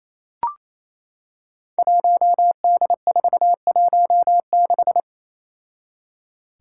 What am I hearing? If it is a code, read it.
Morse code, heard as a keyed tone, about 27 wpm, 700 Hz. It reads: 1D416
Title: Five Challenging Letter-Number Combinations 28wpm